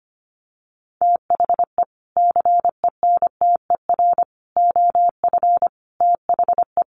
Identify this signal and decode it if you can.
Morse code, heard as a keyed tone, about 25 wpm, 700 Hz. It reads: THE CENTER OF THE
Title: Sets of 5 Words Encore 25wpm